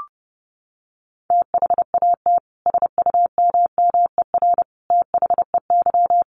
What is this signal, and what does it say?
Morse code, heard as a keyed tone, about 29 wpm, 700 Hz. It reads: THAT SUMMER THEY
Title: Sentences from Top 1100 Words 30wpm